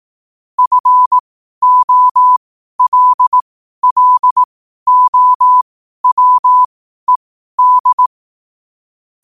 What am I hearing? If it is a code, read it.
Morse code, heard as a keyed tone, about 18 wpm, spaced as for 12 wpm, 1000 Hz. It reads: FOLLOWED
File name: qotdfiles / 12wpm